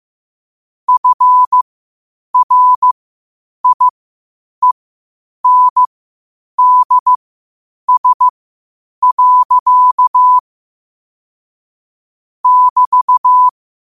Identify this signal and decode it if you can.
Morse code, heard as a keyed tone, about 15 wpm, spaced as for 8 wpm, 1000 Hz. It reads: FRIENDS. <BT>
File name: qotdfiles / 8wpm